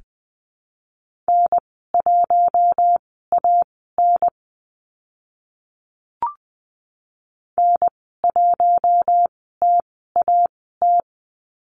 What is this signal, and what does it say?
Morse code, heard as a keyed tone, about 20 wpm, spaced as for 14 wpm, 700 Hz. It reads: N1AN N1TAT
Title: 042 Lesson 8g - Number 1 - Call Signs 20wpm